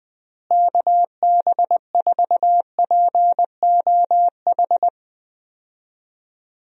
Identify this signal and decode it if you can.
Morse code, heard as a keyed tone, about 20 wpm, 700 Hz. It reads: KB4POH
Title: USA Call Signs - 6 Characters - Repeated 3x - 20wpm